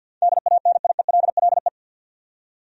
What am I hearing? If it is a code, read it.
Morse code, heard as a keyed tone, about 47 wpm, 700 Hz. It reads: DANIELLE
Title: USA Common Names - 50wpm